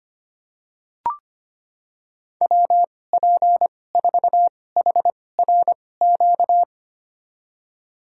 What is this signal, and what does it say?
Morse code, heard as a keyed tone, about 25 wpm, spaced as for 18 wpm, 700 Hz. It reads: WP4HRQ
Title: USA Call Signs - 2x Character Spacing - 25wpm